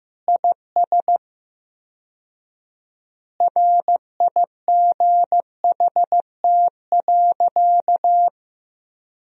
Morse code, heard as a keyed tone, about 15 wpm, 700 Hz. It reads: IS RIGHT.
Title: 4x Word-Spacing - Sentences from Top 200 Words - 15wpm